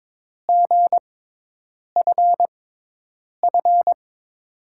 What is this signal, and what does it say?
Morse code, heard as a keyed tone, about 22 wpm, spaced as for 8 wpm, 700 Hz. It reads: GFF
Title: USA Call Signs - 6 Characters - 6x Character Spacing - 22wpm